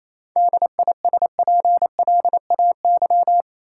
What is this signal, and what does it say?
Morse code, heard as a keyed tone, about 28 wpm, 700 Hz. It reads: DISPLAY
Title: Sentences from Top 1200 Words - New Word Drill Alt - 28wpm